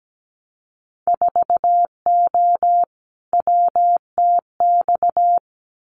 Morse code, heard as a keyed tone, about 17 wpm, 700 Hz. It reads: 4O WTX
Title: Winter Field Day Exchanges Encore 17wpm